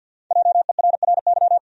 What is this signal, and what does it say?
Morse code, heard as a keyed tone, about 52 wpm, 700 Hz. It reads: JERRY
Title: USA Common Names - Max 5 Letters - 50wpm